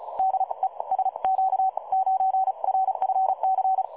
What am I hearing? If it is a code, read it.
Morse code, heard as a keyed tone, about 43 wpm, spaced as for 33 wpm, 750 Hz. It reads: DE SQ9RFC
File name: dlazaawansowanych / nagraniazpasma / websdr recording 3565.0kHz